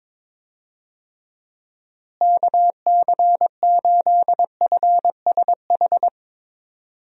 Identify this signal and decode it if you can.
Morse code, heard as a keyed tone, about 21 wpm, 700 Hz. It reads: KC8FSH
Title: USA Call Signs - 6 Characters - Repeated 2x - 22wpm